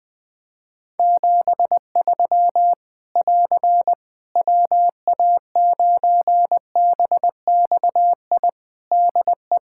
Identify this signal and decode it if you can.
Morse code, heard as a keyed tone, about 20 wpm, 700 Hz. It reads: 73 <AR> WA9BXI DE
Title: K1USN SST QSO Practice 20wpm